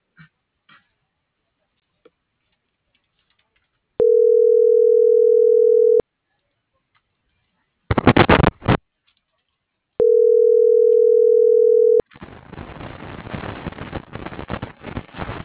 Ringback tone (North America), 440 + 480 Hz.